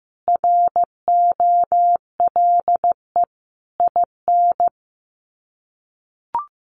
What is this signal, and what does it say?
Morse code, heard as a keyed tone, about 15 wpm, 700 Hz. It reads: ROLE IN
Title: Sets of 5 Words Encore2 15wpm